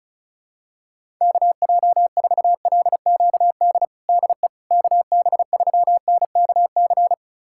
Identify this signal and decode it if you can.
Morse code, heard as a keyed tone, about 36 wpm, 700 Hz. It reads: KJ4LQD DE KB3NKC
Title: USA Callsign DE Callsign - 6 Characters - 35wpm